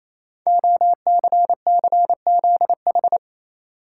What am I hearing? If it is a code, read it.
Morse code, heard as a keyed tone, about 27 wpm, 700 Hz. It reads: OCCZH
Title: Code Groups - 5 Letters - 28wpm